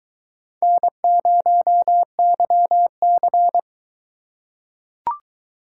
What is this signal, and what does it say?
Morse code, heard as a keyed tone, about 21 wpm, 700 Hz. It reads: N0YC